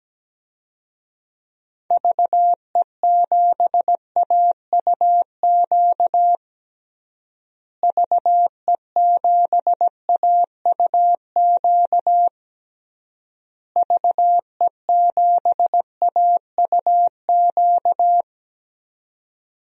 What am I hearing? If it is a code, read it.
Morse code, heard as a keyed tone, about 17 wpm, 700 Hz. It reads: VE7AUQ VE7AUQ VE7AUQ
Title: DXCC-100 Call Signs - 6 Characters - Repeated 3x - 17wpm